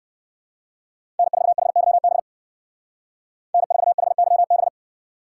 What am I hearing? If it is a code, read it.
Morse code, heard as a keyed tone, about 58 wpm, 700 Hz. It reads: N4HYB N4HYB